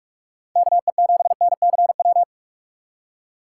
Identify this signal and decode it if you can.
Morse code, heard as a keyed tone, about 46 wpm, 700 Hz. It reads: KE7NCW
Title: USA Call Signs - 6 Characters - 4x Character Spacing - 45wpm